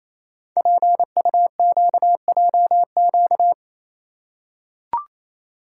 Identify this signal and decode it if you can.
Morse code, heard as a keyed tone, about 29 wpm, 700 Hz. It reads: PUQJQ
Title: Code Groups - 5 Letters - 28wpm